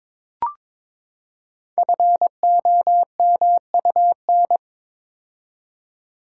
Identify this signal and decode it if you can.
Morse code, heard as a keyed tone, about 22 wpm, 700 Hz. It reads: FOMUN
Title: Code Groups - 5 Letters - 22wpm